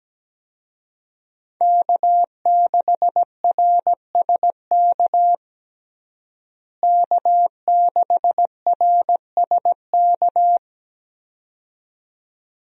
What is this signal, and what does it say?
Morse code, heard as a keyed tone, about 17 wpm, 700 Hz. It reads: K6RSK K6RSK
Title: USA Call Signs - 5 Characters - Repeated 2x - 17wpm